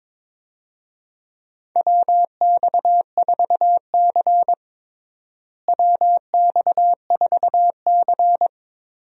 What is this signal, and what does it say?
Morse code, heard as a keyed tone, about 22 wpm, 700 Hz. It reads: WX4C WX4C
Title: USA Call Signs - 4 Characters - Repeated 2x - 22wpm